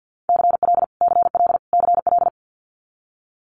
Morse code, heard as a keyed tone, about 50 wpm, 700 Hz. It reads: CL CL CL
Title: Multi-letter Phonemes Repeated 3x 50wpm